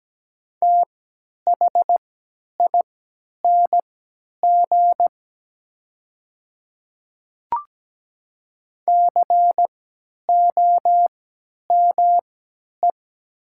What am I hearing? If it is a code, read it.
Morse code, heard as a keyed tone, about 17 wpm, spaced as for 10 wpm, 700 Hz. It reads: THING COME
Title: 3x Character Spacing - Top 100 Words - 17wpm